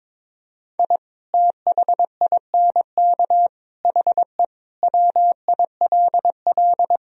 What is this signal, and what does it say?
Morse code, heard as a keyed tone, about 21 wpm, 700 Hz. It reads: I THINK HE WILL
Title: Sentences from Top 1000 Words - Review Part 3 - 22wpm